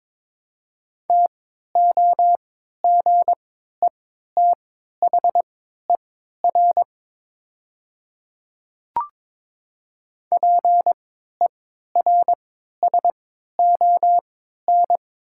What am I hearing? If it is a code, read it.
Morse code, heard as a keyed tone, about 22 wpm, spaced as for 13 wpm, 700 Hz. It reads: TOGETHER PERSON
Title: 3x Character Spacing - Top 500 Words - 22wpm